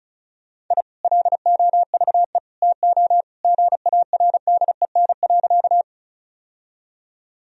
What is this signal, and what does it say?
Morse code, heard as a keyed tone, about 36 wpm, 700 Hz. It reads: I LOVE TO GARDEN.
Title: Sentences from Top 1000 Words - Review Part 4 - 35wpm